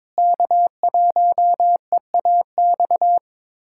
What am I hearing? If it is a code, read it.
Morse code, heard as a keyed tone, about 22 wpm, 700 Hz. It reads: K1EAX
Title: USA Callsign DE Callsign 22wpm